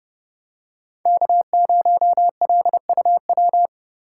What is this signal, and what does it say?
Morse code, heard as a keyed tone, about 30 wpm, 700 Hz. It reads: K0LUW
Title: K1USN SST QSO Practice - 30wpm